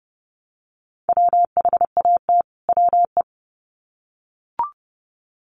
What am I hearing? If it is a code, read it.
Morse code, heard as a keyed tone, about 30 wpm, 700 Hz. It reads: WHAT WE
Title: Sets of 2 Words 30wpm